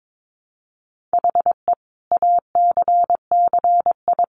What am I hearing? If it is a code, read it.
Morse code, heard as a keyed tone, about 22 wpm, 700 Hz. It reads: HE ACCI
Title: Additional Easy Phrases 22wpm